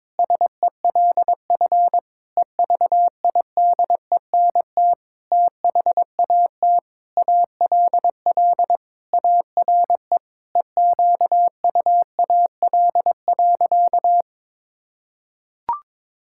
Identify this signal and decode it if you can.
Morse code, heard as a keyed tone, about 22 wpm, 700 Hz. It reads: SELF EVIDENT THAT ALL ARE EQUAL.
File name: Sentences with 9-Word Phrases - 22wpm